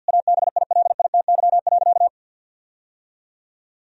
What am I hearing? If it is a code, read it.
Morse code, heard as a keyed tone, about 47 wpm, 700 Hz. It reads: ABILITY.
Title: Sentences from Top 1000 Words - Review Part 3 - 50wpm